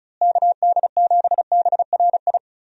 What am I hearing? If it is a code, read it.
Morse code, heard as a keyed tone, about 36 wpm, 700 Hz. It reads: KD7BRI
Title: USA Call Signs - 6 Characters - Repeated 3x - 35wpm